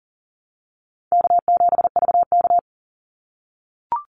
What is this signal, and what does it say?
Morse code, heard as a keyed tone, about 40 wpm, 700 Hz. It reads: K7VK
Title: NA SOTA Activator Callsigns 40wpm